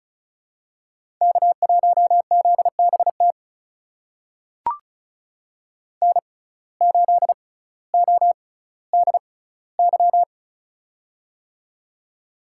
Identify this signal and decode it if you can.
Morse code, heard as a keyed tone, about 36 wpm, spaced as for 18 wpm, 700 Hz. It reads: K1ZBT N8ODY
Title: USA Call Signs - 5 Characters - 6x Character Spacing - 35wpm